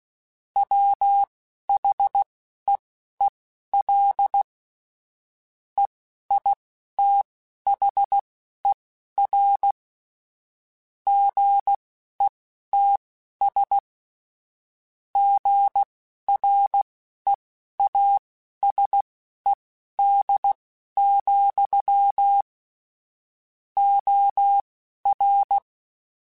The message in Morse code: WHEEL EITHER GETS GREASED, OR

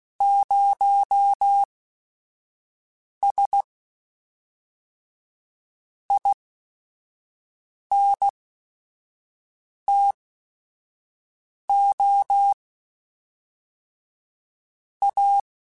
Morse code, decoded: 0S INTO A

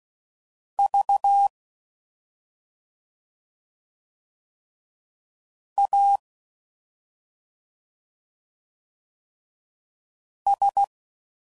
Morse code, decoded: V A S

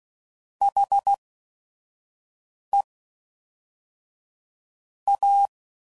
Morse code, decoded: HEA

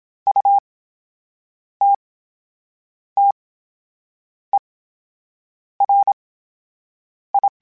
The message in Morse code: U T T E R I

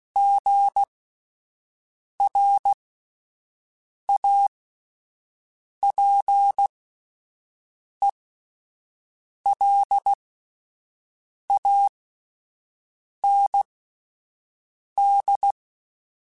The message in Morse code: GRAPELAND

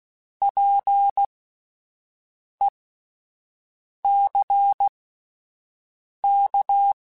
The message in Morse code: PECK

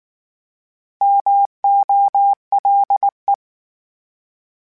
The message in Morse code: MOLE

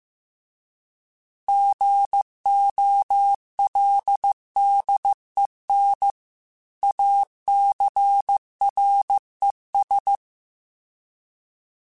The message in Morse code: GOLDEN ACRES

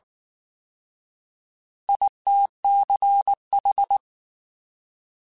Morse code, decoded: ITCH

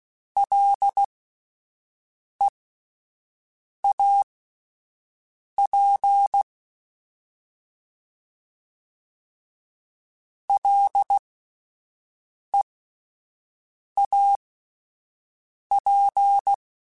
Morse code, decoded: LEAP LEAP

